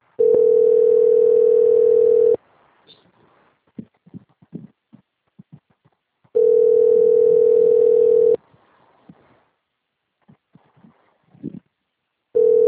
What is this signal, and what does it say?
Ringback tone (North America), 440 + 480 Hz.
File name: out- - - - - .61202